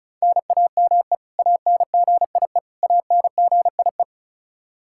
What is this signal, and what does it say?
Morse code, heard as a keyed tone, about 34 wpm, 700 Hz. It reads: NAME ANGIE ANGIE
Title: Practice QSO - Exchange 1 - Beginning - 35wpm